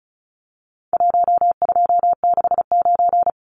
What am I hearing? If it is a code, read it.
Morse code, heard as a keyed tone, about 34 wpm, 700 Hz. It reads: 1269
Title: Numbers - Contest Style - 4-Digit 35wpm